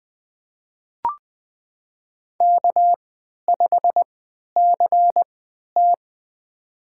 Morse code, heard as a keyed tone, about 20 wpm, spaced as for 11 wpm, 700 Hz. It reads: K5CT
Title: USA Call Signs - 4 Characters - 3x Character Spacing - 20wpm